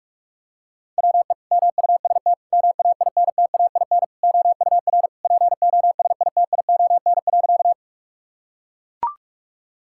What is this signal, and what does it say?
Morse code, heard as a keyed tone, about 41 wpm, 700 Hz. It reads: WE MUST MAINTAIN OUR POSITION.
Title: Sentences from Top 1000 Words - Review Part 5 - 45wpm